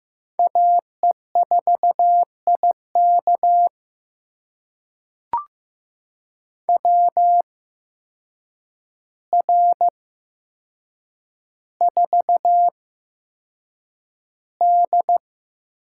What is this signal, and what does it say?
Morse code, heard as a keyed tone, about 15 wpm, spaced as for 7 wpm, 700 Hz. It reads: AE4IK WR4D
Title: USA Call Signs - 8x Character Spacing - 15wpm